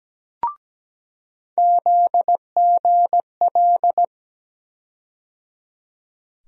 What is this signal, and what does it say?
Morse code, heard as a keyed tone, about 17 wpm, 700 Hz. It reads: ZGL
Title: USA Call Sign Postfixes - 3 Letters 17wpm